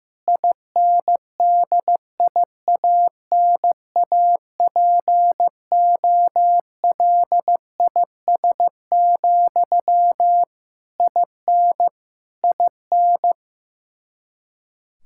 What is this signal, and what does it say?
Morse code, heard as a keyed tone, about 15 wpm, 700 Hz. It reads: INDIANAPOLIS, IN IN
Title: Practice QSO - Exchange 1 - Beginning - 15wpm